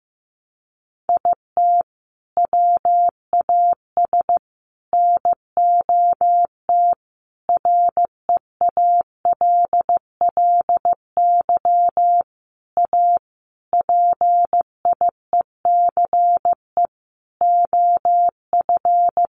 Morse code, heard as a keyed tone, about 15 wpm, 700 Hz. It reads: IT WAS NOT REALLY A PIECE OF
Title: Sentences from Top 600 Words 15wpm